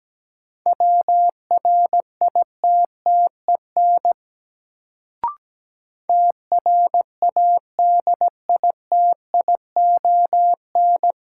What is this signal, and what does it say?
Morse code, heard as a keyed tone, about 17 wpm, 700 Hz. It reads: WRITTEN TRADITION
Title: Sentences from Top 1500 Words - New Word Drill Alt - 17wpm